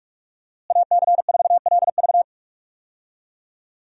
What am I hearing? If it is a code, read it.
Morse code, heard as a keyed tone, about 46 wpm, 700 Hz. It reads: AC4LV
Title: USA Call Signs - 5 Characters - Repeated 2x - 45wpm